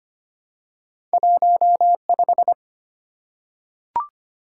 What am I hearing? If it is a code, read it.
Morse code, heard as a keyed tone, about 25 wpm, 700 Hz. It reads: 15
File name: Two Letter-Number Combinations - 25wpm